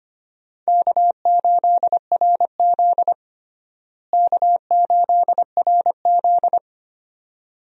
Morse code, heard as a keyed tone, about 25 wpm, 700 Hz. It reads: K8RZ K8RZ
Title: USA Call Signs - 4 Characters - Repeated 2x - 25wpm